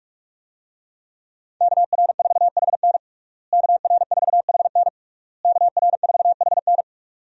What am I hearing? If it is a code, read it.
Morse code, heard as a keyed tone, about 46 wpm, 700 Hz. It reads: KR4HN KR4HN KR4HN
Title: USA Call Signs - 5 Characters - Repeated 3x - 45wpm